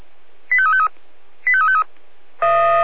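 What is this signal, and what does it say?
Five-tone selective call (ZVEI), heard as quick runs of five tones, tones of about 70 ms. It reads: 85353 85353
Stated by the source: Five-tone ZVEI selective call received on SDR.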